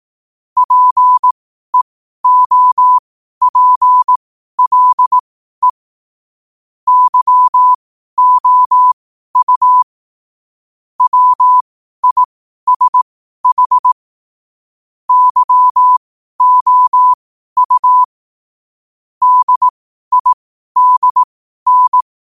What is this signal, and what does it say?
Morse code, heard as a keyed tone, about 18 wpm, spaced as for 12 wpm, 1000 Hz. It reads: PEOPLE YOU WISH YOU DIDN
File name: qotdfiles / 12wpm